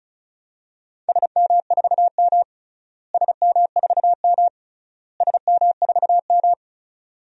Morse code, heard as a keyed tone, about 34 wpm, 700 Hz. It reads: SM4M SM4M SM4M
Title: DXCC-100 Call Signs - 4 Characters - Repeated 3x - 35wpm